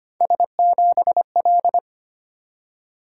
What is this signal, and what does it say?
Morse code, heard as a keyed tone, about 25 wpm, 700 Hz. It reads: S7L
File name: Three Letter-Number Combinations - 25wpm